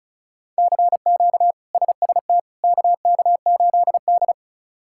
Morse code, heard as a keyed tone, about 34 wpm, 700 Hz. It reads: CQ SST KK8D
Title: K1USN SST QSO Practice - 35wpm